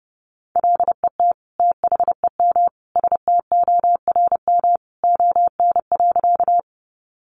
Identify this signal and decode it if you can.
Morse code, heard as a keyed tone, about 30 wpm, 700 Hz. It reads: LET THEM STORM ON.
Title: Sentences from Top 1500 Words 30wpm